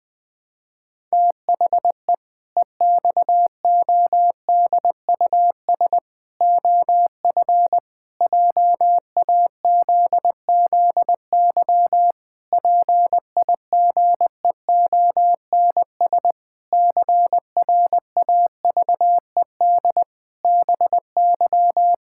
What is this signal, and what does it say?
Morse code, heard as a keyed tone, about 20 wpm, 700 Hz. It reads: THE EXODUS OF JAZZY PIGEONS CRAVED BY
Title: Pangram Challenge v2 - 20wpm